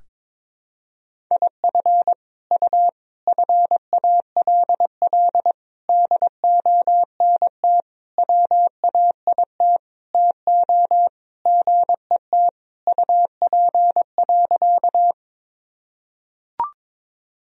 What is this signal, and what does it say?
Morse code, heard as a keyed tone, about 22 wpm, 700 Hz. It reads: IF U FALL DONT WAIT TO GET UP.
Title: Sentences from Top 500 Words and Abbreviations 22wpm